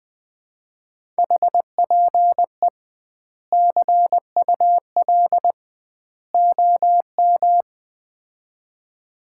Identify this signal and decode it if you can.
Morse code, heard as a keyed tone, about 20 wpm, 700 Hz. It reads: HPE CUL OM
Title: Common QSO Phrases - 2x Word-Spacing - 20wpm